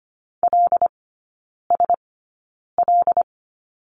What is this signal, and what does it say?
Morse code, heard as a keyed tone, about 25 wpm, spaced as for 9 wpm, 700 Hz. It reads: LSL
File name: Lesson 17e - Letter u - Three Characters 25wpm